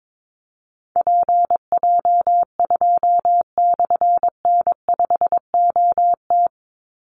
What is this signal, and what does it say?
Morse code, heard as a keyed tone, about 22 wpm, 700 Hz. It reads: PJ2/N5OT
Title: Call Signs - 22wpm